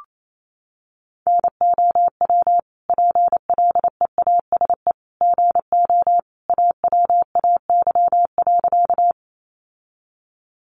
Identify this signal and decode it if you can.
Morse code, heard as a keyed tone, about 28 wpm, 700 Hz. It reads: NOW PLEASE GO AWAY.
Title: Sentences from Top 1500 Words 28wpm